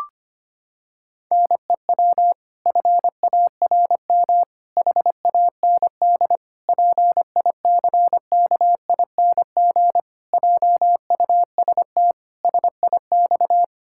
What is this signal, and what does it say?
Morse code, heard as a keyed tone, about 25 wpm, 700 Hz. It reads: NEW FARM HAND PICKING JUST SIX
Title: Pangram Challenge v2 - 25wpm